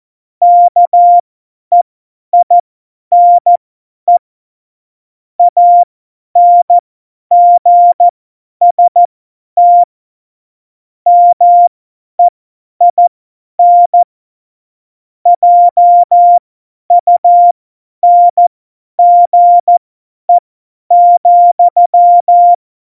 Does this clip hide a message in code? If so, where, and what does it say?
Yes: Morse code from 0.4 to 22.5 s, heard as a keyed tone, about 14 wpm, spaced as for 10 wpm, 700 Hz. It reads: KEINE ANGST MEIN JUNGE,